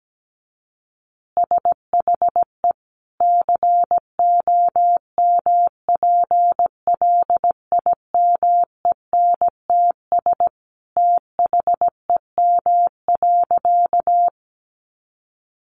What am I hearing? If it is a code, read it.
Morse code, heard as a keyed tone, about 17 wpm, 700 Hz. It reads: SHE COMPLIMENTS THEM.
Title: Easy Phrases 17wpm